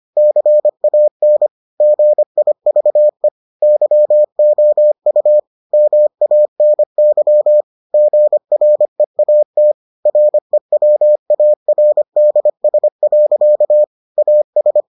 Morse code, heard as a keyed tone, about 25 wpm, 600 Hz. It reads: CAN GIVE YOU MANY GREAT REWARDS. AS